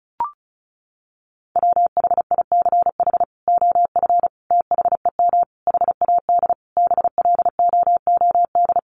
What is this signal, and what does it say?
Morse code, heard as a keyed tone, about 34 wpm, 700 Hz. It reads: WHICH OF THEM HAD BLOOD
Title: Sentences from Top 1100 Words 35wpm